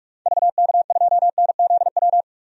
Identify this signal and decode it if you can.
Morse code, heard as a keyed tone, about 42 wpm, 700 Hz. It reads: VK2NZW
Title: DXCC-100 Call Signs - 6 Characters - 45wpm